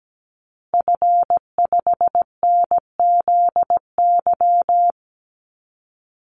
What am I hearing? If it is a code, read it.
Morse code, heard as a keyed tone, about 17 wpm, 700 Hz. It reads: F5NZY
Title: Call Signs Repeated 2x - 17wpm